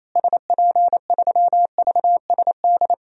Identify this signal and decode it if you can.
Morse code, heard as a keyed tone, about 28 wpm, 700 Hz. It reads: SP3VSD
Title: DXCC-100 Call Signs - 6 Characters - Repeated 3x - 28wpm